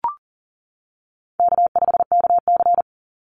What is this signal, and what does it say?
Morse code, heard as a keyed tone, about 40 wpm, 700 Hz. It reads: K5KC
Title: Easy Call Signs - 40wpm